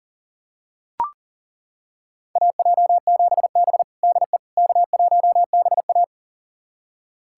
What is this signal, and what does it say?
Morse code, heard as a keyed tone, about 40 wpm, 700 Hz. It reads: AJ7B DE K1BA
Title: USA Callsign DE Callsign - 4 Characters - 40wpm